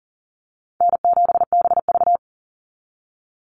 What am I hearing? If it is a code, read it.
Morse code, heard as a keyed tone, about 40 wpm, 700 Hz. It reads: N7BV
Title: Easy Call Signs - 40wpm